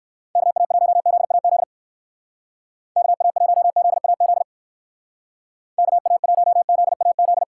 Morse code, heard as a keyed tone, about 59 wpm, 700 Hz. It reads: KA1ZAZ KA1ZAZ KA1ZAZ